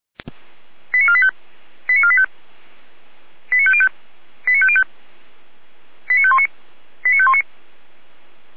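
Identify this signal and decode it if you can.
Five-tone selective call (ZVEI), heard as quick runs of five tones, tones of about 70 ms. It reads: 89485 89485 89505 89505 89519 89519
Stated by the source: Five-tone ZVEI selective call received by SDR at feuerwehr-orlamuende.org (Orlamünde fire brigade).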